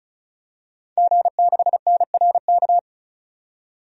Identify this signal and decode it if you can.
Morse code, heard as a keyed tone, about 34 wpm, 700 Hz. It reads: G6NRK
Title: DXCC-100 Call Signs - 5 Characters - 35wpm